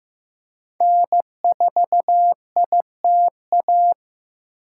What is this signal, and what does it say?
Morse code, heard as a keyed tone, about 15 wpm, 700 Hz. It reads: N4ITA
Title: USA Call Signs - 5 Characters - 2x Character Spacing - 15wpm